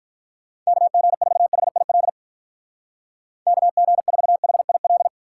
Morse code, heard as a keyed tone, about 44 wpm, 700 Hz. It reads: KG4HIL KG4HIL
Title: Speed Racing - USA Call Signs 35wpm